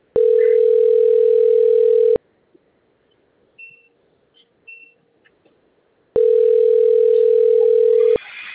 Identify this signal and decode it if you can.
Ringback tone (North America), 440 + 480 Hz.